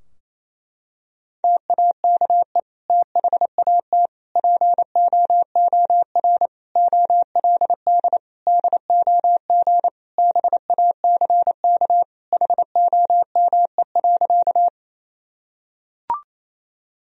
Morse code, heard as a keyed tone, about 28 wpm, 700 Hz. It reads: TAKE THAT POOR OLD DOG BACK HOME.
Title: Sentences from Top 1000 Words - Review Part 2 - 28wpm